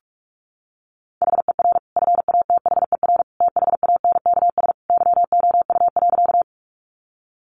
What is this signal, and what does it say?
Morse code, heard as a keyed tone, about 44 wpm, 700 Hz. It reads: HER FATHER THANKS YOU.
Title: Sentences from Top 600 Words 45wpm